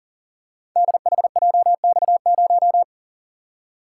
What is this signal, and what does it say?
Morse code, heard as a keyed tone, about 39 wpm, 700 Hz. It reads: DHJX0
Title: Code Groups - 5 Letter:Numbers - 40wpm